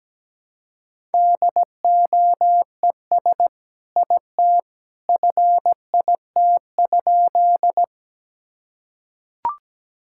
Morse code, heard as a keyed tone, about 17 wpm, 700 Hz. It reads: DOES IT FIT?
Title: Sentences from Top 1000 Words - Review Part 5 - 17wpm